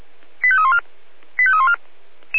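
Five-tone selective call (ZVEI), heard as quick runs of five tones, tones of about 70 ms. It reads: 85314 85314
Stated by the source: ZVEI five-tone call received on an SDR.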